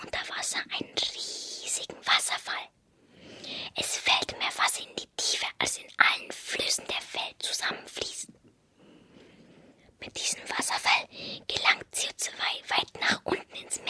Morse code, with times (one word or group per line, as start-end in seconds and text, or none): none